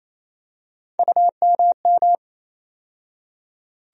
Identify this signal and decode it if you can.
Morse code, heard as a keyed tone, about 28 wpm, 700 Hz. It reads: UMM
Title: USA Call Sign Postfixes - 3 Letters 28wpm